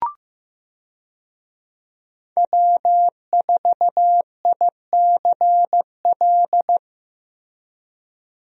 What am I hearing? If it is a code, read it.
Morse code, heard as a keyed tone, about 16 wpm, 700 Hz. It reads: W4ICL
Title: USA Call Signs - 5 Characters - Repeated 3x - 15wpm